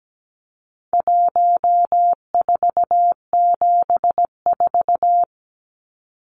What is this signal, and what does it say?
Morse code, heard as a keyed tone, about 17 wpm, 700 Hz. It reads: 1474
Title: Numbers - 4-Digit 17wpm